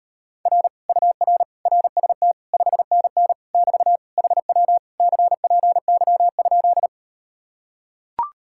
Morse code, heard as a keyed tone, about 39 wpm, 700 Hz. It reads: R UR RST 5NN <BT> HW CPY?